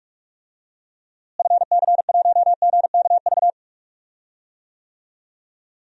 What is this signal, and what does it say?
Morse code, heard as a keyed tone, about 46 wpm, 700 Hz. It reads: FC1GKV
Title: DXCC-100 Call Signs - 6 Characters - Repeated 2x - 45wpm